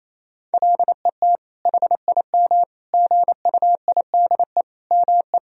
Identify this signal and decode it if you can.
Morse code, heard as a keyed tone, about 28 wpm, 700 Hz. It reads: LET HIM GUIDE ME
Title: Sentences from Top 1000 Words - Review Part 3 - 28wpm